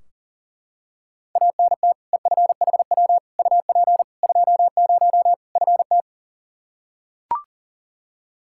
Morse code, heard as a keyed tone, about 37 wpm, 700 Hz. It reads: ANT EFHW UP 20 FT
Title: Practice QSO - Exchange 2 - Info - 40wpm